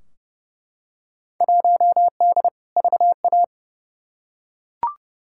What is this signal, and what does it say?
Morse code, heard as a keyed tone, about 30 wpm, 700 Hz. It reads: 1D VA
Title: Field Day Exchanges - 30wpm